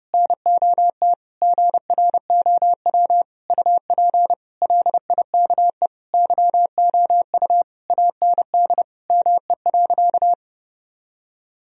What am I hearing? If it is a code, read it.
Morse code, heard as a keyed tone, about 32 wpm, 700 Hz. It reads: NOT GROW UP LIKE YOU AND ME.
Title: Sentences from Top 1000 Words - Review Part 1 - 30wpm